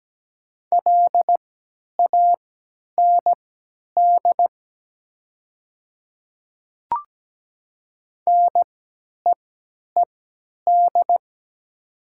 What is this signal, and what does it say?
Morse code, heard as a keyed tone, about 17 wpm, spaced as for 10 wpm, 700 Hz. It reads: LAND NEED
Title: 3x Character Spacing - Top 200 Words - 17wpm